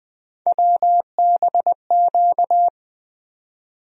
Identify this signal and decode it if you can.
Morse code, heard as a keyed tone, about 20 wpm, 700 Hz. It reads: WBQ
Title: USA Call Sign Postfixes - 3 Letters 20wpm